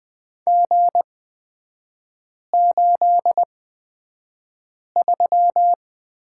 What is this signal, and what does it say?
Morse code, heard as a keyed tone, about 20 wpm, spaced as for 5 wpm, 700 Hz. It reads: G83
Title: 243 Lesson 39e - Letter z - Three Characters 20wpm